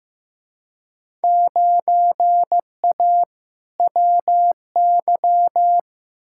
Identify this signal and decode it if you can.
Morse code, heard as a keyed tone, about 15 wpm, 700 Hz. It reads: 9A WY
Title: Field Day Exchanges Encore - 15wpm